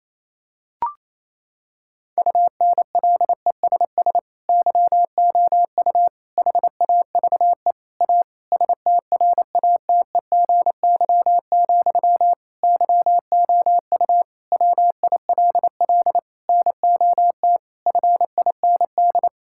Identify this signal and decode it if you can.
Morse code, heard as a keyed tone, about 28 wpm, 700 Hz. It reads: UNLESS YOU HAVE A STRATEGY, YOU WILL NOT FIND
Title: Sentences from Top 1000 Words - Review Part 5 - 28wpm